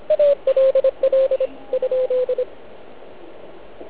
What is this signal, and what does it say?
Morse code, heard as a keyed tone, about 26 wpm, 470 Hz. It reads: LL ?